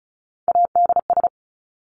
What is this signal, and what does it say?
Morse code, heard as a keyed tone, about 34 wpm, 700 Hz. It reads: ADS
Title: Abbreviations 35wpm